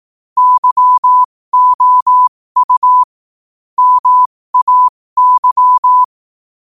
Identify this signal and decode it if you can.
Morse code, heard as a keyed tone, about 18 wpm, spaced as for 15 wpm, 1000 Hz. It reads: YOU MAY
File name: qotdfiles / 15wpm